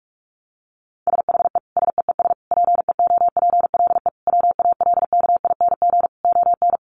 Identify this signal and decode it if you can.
Morse code, heard as a keyed tone, about 44 wpm, 700 Hz. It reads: SHE SEES PEOPLE WALKING ON
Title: Easy Phrases 45wpm